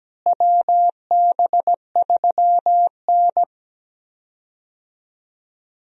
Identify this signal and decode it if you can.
Morse code, heard as a keyed tone, about 17 wpm, 700 Hz. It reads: WB3N
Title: USA Call Signs - 4 Characters - Repeated 2x - 17wpm